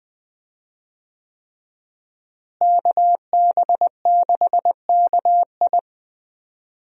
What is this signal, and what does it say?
Morse code, heard as a keyed tone, about 20 wpm, 700 Hz. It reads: KB6KI
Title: USA Call Signs - 5 Characters - Repeated 3x - 20wpm